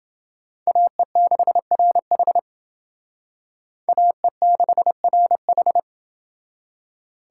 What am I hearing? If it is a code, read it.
Morse code, heard as a keyed tone, about 28 wpm, 700 Hz. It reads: AE6RH AE6RH